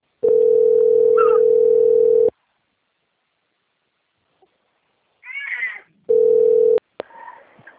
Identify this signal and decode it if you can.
Ringback tone (North America), 440 + 480 Hz.